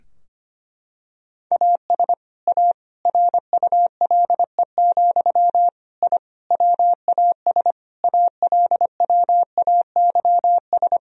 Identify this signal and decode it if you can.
Morse code, heard as a keyed tone, about 25 wpm, 700 Hz. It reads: AS A RULE, I WAS ALWAYS
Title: Sentences from Top 1000 Words - Review Part 2 - 25wpm